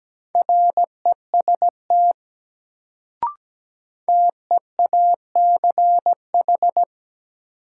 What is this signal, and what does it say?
Morse code, heard as a keyed tone, about 17 wpm, 700 Hz. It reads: REST TEACH
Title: Sentences from Top 500 Words - New Word Drill - 17wpm